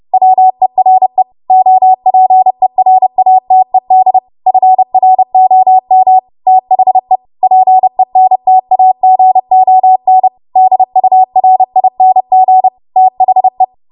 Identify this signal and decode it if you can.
Morse code, heard as a keyed tone, about 30 wpm, 750 Hz. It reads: WERE OPERATED FROM THE PENTAGON DURING THE
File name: Archive / 30 WPM / 30WPM